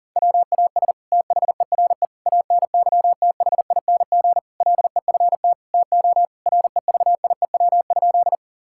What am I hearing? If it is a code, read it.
Morse code, heard as a keyed tone, about 37 wpm, 700 Hz. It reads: WAS THERE ANYTHING LEFT TO REVIEW?
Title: Sentences from Top 1000 Words - Review Part 6 - 40wpm